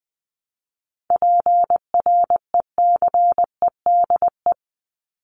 Morse code, heard as a keyed tone, about 20 wpm, 700 Hz. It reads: PRECEDE
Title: ING Suffix - Top 100 Words with Base Words 20wpm